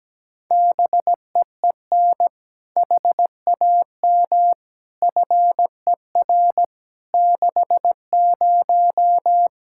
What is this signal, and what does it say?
Morse code, heard as a keyed tone, about 17 wpm, 700 Hz. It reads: BEEN HAM FER 60
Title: Practice QSO - Exchange 3 - More Info - 17wpm